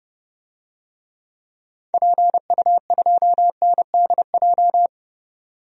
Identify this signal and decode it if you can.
Morse code, heard as a keyed tone, about 30 wpm, 700 Hz. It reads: PU2NDJ
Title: DXCC-100 Call Signs - 6 Characters - 30wpm